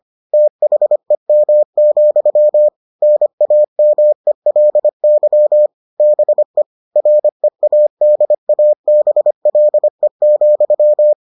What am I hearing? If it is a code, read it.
Morse code, heard as a keyed tone, about 25 wpm, 600 Hz. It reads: THEM, NAMELY BE READABLE,